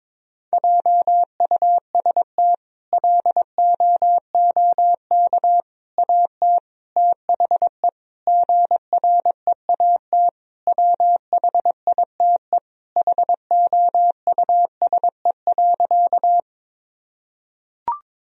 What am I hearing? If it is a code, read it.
Morse code, heard as a keyed tone, about 22 wpm, 700 Hz. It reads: JUST LOOK AT THE GREAT WHITE HOUSE.
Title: Sentences from Top 1000 Words - Review Part 1 - 22wpm